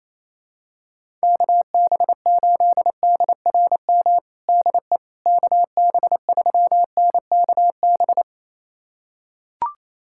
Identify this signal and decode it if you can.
Morse code, heard as a keyed tone, about 28 wpm, 700 Hz. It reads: KB8DRM DE KB3NKB
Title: USA Callsign DE Callsign - 6 Characters - 28wpm